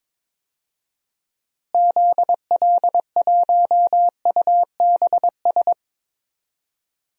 Morse code, heard as a keyed tone, about 22 wpm, 700 Hz. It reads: ZL1UBS
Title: DXCC-100 Call Signs - 6 Characters - Repeated 3x - 22wpm